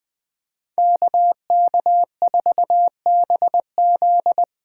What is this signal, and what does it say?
Morse code, heard as a keyed tone, about 20 wpm, 700 Hz. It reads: KK4BZ
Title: Speed Racing - USA 2x2 Call Signs 17wpm